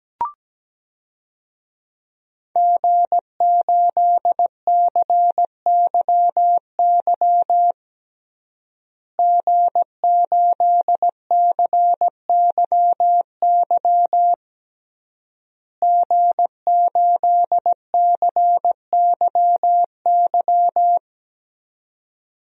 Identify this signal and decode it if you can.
Morse code, heard as a keyed tone, about 17 wpm, 700 Hz. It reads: G8CYY G8CYY G8CYY
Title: DXCC-100 Call Signs - 5 Characters - Repeated 3x - 17wpm